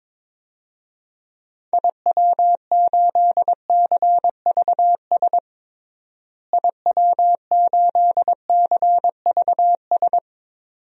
Morse code, heard as a keyed tone, about 22 wpm, 700 Hz. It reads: IW8CVS IW8CVS
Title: DXCC-100 Call Signs - 6 Characters - Repeated 2x - 22wpm